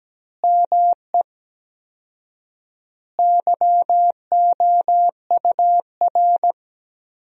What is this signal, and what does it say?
Morse code, heard as a keyed tone, about 17 wpm, 700 Hz. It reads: ME YOUR
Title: 4x Word-Spacing - Sentences from Top 200 Words - 17wpm